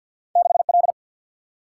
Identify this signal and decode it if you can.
Morse code, heard as a keyed tone, about 51 wpm, 700 Hz. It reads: 6L
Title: Single-Letter Single-Number 50wpm